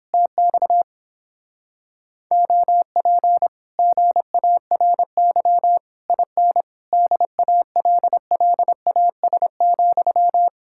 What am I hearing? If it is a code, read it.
Morse code, heard as a keyed tone, about 29 wpm, 700 Hz. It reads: TX OP GARY IN DALLAS,